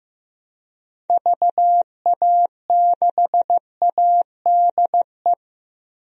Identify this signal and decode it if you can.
Morse code, heard as a keyed tone, about 15 wpm, 700 Hz. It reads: VA6ADE
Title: DXCC-100 Call Signs - 6 Characters - Repeated 2x - 15wpm